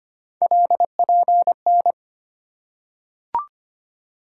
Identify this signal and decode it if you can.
Morse code, heard as a keyed tone, about 25 wpm, 700 Hz. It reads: LPN
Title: Practice QSO - Exchange 3 - More Info - 25wpm